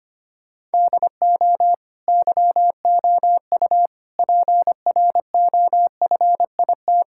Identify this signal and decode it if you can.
Morse code, heard as a keyed tone, about 25 wpm, 700 Hz. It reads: DO YOU PROFIT
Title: Sentences from Top 1000 Words - Review Part 5 - 25wpm